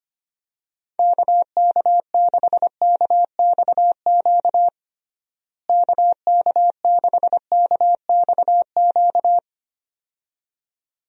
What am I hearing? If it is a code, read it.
Morse code, heard as a keyed tone, about 25 wpm, 700 Hz. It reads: KK6KXQ KK6KXQ
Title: USA Call Signs - 6 Characters - Repeated 2x - 25wpm